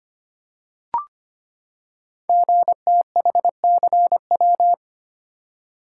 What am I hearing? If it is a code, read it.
Morse code, heard as a keyed tone, about 25 wpm, 700 Hz. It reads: GTHCW
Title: Code Groups - 5 Letters - 25wpm